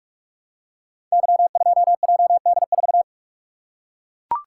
Morse code, heard as a keyed tone, about 44 wpm, 700 Hz. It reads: Y2JD4
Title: Code Groups - 5 Letter:Numbers - 45wpm